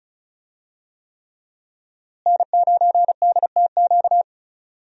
Morse code, heard as a keyed tone, about 34 wpm, 700 Hz. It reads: N9DTQ
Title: USA Call Signs - 5 Characters - Repeated 3x - 35wpm